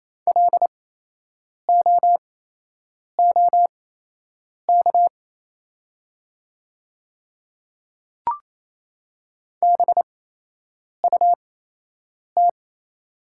Morse code, heard as a keyed tone, about 28 wpm, spaced as for 8 wpm, 700 Hz. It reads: LOOK BUT
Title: 8x Character Spacing - Top 100 Words - 28wpm